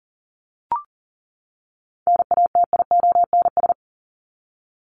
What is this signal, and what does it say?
Morse code, heard as a keyed tone, about 40 wpm, 700 Hz. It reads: NATIONS
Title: ES:S Suffix - Top 100 Words with Base Words 40wpm